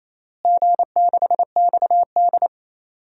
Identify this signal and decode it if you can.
Morse code, heard as a keyed tone, about 27 wpm, 700 Hz. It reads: G6XD
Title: DXCC-100 Call Signs - 4 Characters - Repeated 3x - 28wpm